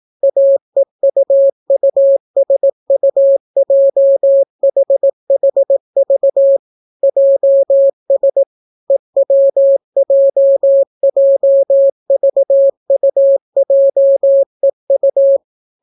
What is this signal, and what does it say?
Morse code, heard as a keyed tone, about 18 wpm, 550 Hz. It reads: AEUUSUJHHV JS EWJJVUJEU